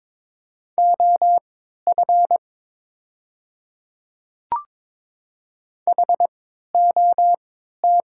Morse code, heard as a keyed tone, about 22 wpm, spaced as for 12 wpm, 700 Hz. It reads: OF HOT
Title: 3x Character Spacing - Top 200 Words - 22wpm